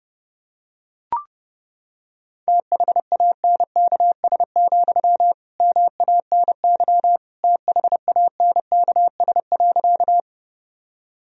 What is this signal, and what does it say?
Morse code, heard as a keyed tone, about 32 wpm, 700 Hz. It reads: THANKS, MANY THANKS.
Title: Sentences from Top 1000 Words - Review Part 3 - 30wpm